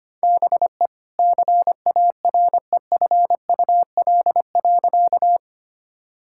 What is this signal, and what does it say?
Morse code, heard as a keyed tone, about 25 wpm, 700 Hz. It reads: BE CAREFUL.
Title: Sentences from Top 1600 Words 25wpm